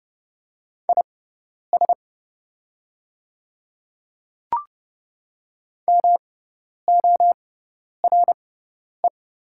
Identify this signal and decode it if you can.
Morse code, heard as a keyed tone, about 30 wpm, spaced as for 10 wpm, 700 Hz. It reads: IS MORE
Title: 6x Character Spacing - Top 100 Words - 30wpm